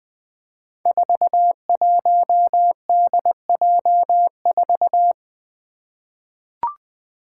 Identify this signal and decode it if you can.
Morse code, heard as a keyed tone, about 20 wpm, 700 Hz. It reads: 41DJ4
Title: Five Challenging Letter-Number Combinations 20wpm